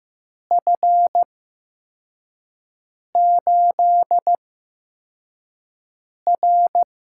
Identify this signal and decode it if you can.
Morse code, heard as a keyed tone, about 15 wpm, spaced as for 4 wpm, 700 Hz. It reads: F8R
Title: USA Call Signs - 4 Characters - 8x Character Spacing - 15wpm